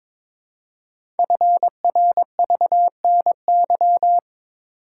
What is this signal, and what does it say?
Morse code, heard as a keyed tone, about 22 wpm, 700 Hz. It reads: FRVNY
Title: Code Groups - 5 Letters - 22wpm